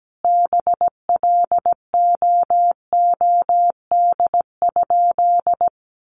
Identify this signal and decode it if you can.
Morse code, heard as a keyed tone, about 17 wpm, 700 Hz. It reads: BLOOD?
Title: Sentences from Top 1400 Words 17wpm